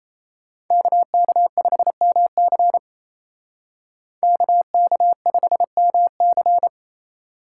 Morse code, heard as a keyed tone, about 37 wpm, 700 Hz. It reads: KK5MC KK5MC